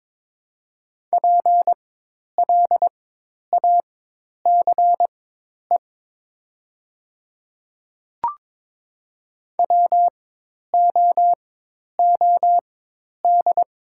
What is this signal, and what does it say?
Morse code, heard as a keyed tone, about 22 wpm, spaced as for 10 wpm, 700 Hz. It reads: PLACE WOOD
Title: 4x Character Spacing - Top 300 Words - 22wpm